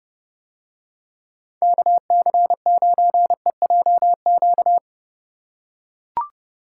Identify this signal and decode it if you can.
Morse code, heard as a keyed tone, about 30 wpm, 700 Hz. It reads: KC9EJQ